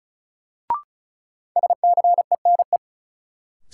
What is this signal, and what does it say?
Morse code, heard as a keyed tone, about 34 wpm, 700 Hz. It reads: SCENE
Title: Sentences from Top 1100 Words - New Word Drill Alt - 35wpm